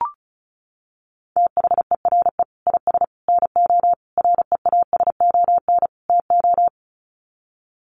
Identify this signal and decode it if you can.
Morse code, heard as a keyed tone, about 34 wpm, 700 Hz. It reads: THERE IS NO REASON TO
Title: Sets of 5 Words Encore 35wpm